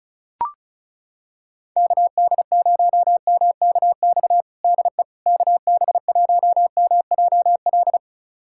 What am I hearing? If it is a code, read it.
Morse code, heard as a keyed tone, about 34 wpm, 690 Hz. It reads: KD0MKX DE KB1MJL
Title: USA Callsign DE Callsign - 6 Characters - 35wpm